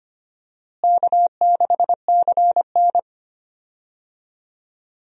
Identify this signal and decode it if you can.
Morse code, heard as a keyed tone, about 25 wpm, 700 Hz. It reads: K6CN
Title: USA Call Signs - 4 Characters - Repeated 3x - 25wpm